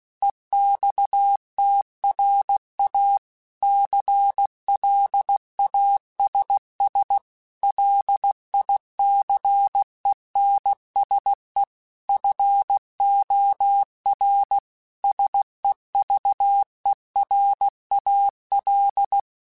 Morse code, heard as a keyed tone, about 16 wpm, 800 Hz. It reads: EXTRA CLASS LICENSE FOR SEVERAL
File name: QSOs / QSOs-mp3s / WA7QOX 2@15